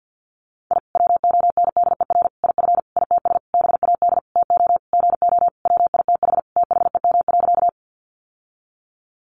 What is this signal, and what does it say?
Morse code, heard as a keyed tone, about 50 wpm, 700 Hz. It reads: I WONDER IF ITS BAD TO GO WITH THEM.
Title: Sentences from Top 700 Words 50wpm